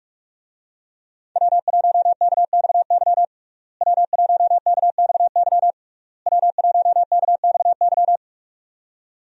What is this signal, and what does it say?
Morse code, heard as a keyed tone, about 46 wpm, 700 Hz. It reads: W1KXY W1KXY W1KXY
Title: USA Call Signs - 5 Characters - Repeated 3x - 45wpm